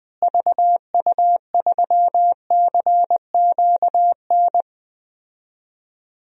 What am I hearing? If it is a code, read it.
Morse code, heard as a keyed tone, about 21 wpm, 700 Hz. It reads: VU3CQN
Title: DXCC-100 Call Signs - 6 Characters - Repeated 3x - 20wpm